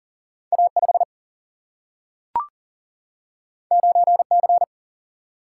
Morse code, heard as a keyed tone, about 40 wpm, 700 Hz. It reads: A5 9C
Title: Single-Letter Single-Number 40wpm